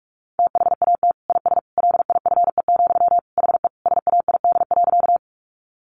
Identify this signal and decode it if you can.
Morse code, heard as a keyed tone, about 44 wpm, 700 Hz. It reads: THAT IS LIFE, HE SAID.
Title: Sentences from Top 900 Words 45wpm